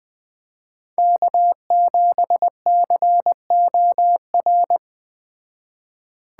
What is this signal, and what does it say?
Morse code, heard as a keyed tone, about 20 wpm, 700 Hz. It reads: K7COR
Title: Speed Racing - USA Call Signs 17wpm